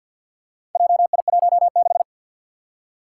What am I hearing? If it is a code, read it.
Morse code, heard as a keyed tone, about 48 wpm, 700 Hz. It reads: JI16
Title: Four Challenging Letter-Number Combinations 50wpm